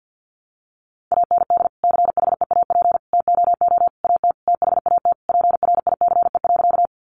Morse code, heard as a keyed tone, about 48 wpm, 700 Hz. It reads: AND CHEAP TOO AT THAT PRICE.
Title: Sentences from Top 1200 Words 50wpm